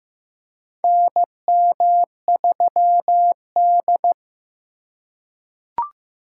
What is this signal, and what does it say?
Morse code, heard as a keyed tone, about 15 wpm, 700 Hz. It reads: NM3D
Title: USA Call Signs - 4x Character Spacing - 15wpm